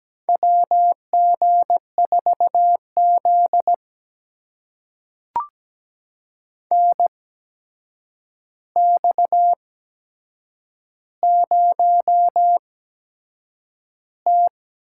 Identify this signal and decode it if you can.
Morse code, heard as a keyed tone, about 17 wpm, spaced as for 7 wpm, 700 Hz. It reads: WG4Z NX0T
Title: USA Call Signs - 4 Characters - 8x Character Spacing - 17wpm